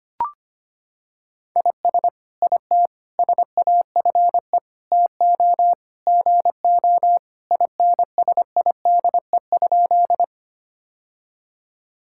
Morse code, heard as a keyed tone, about 25 wpm, 700 Hz. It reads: IS IT SAFE TO GO INSIDE?
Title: Sentences from Top 1000 Words - Review Part 6 - 25wpm